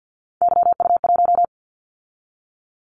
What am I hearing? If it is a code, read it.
Morse code, heard as a keyed tone, about 55 wpm, 700 Hz. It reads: YU1
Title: Callsign Prefixes for Top 25 DXCC - 50wpm